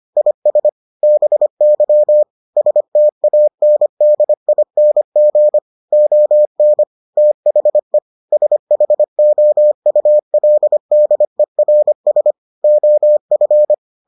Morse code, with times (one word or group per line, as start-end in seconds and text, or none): 0.2-0.7 IS
1.0-2.2 BY
2.6-5.6 STANDING
5.9-6.8 ON
7.2-8.0 THE
8.3-12.3 SHOULDERS
12.6-13.7 OF